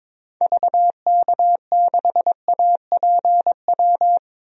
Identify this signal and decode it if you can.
Morse code, heard as a keyed tone, about 22 wpm, 700 Hz. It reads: VK6APW
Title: DXCC-100 Call Signs - 6 Characters - Repeated 3x - 22wpm